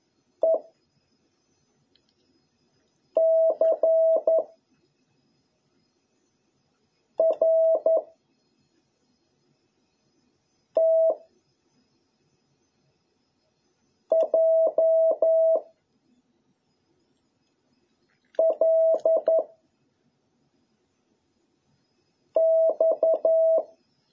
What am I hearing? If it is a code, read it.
Morse code, heard as a keyed tone, about 11 wpm, 650 Hz. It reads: E C R T J L X